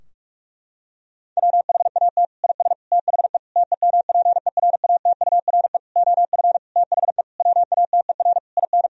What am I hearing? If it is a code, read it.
Morse code, heard as a keyed tone, about 46 wpm, 700 Hz. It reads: WHAT IS THE TEMPERATURE OF THE WATER IN
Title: Sentences from Top 1000 Words - Review Part 4 - 45wpm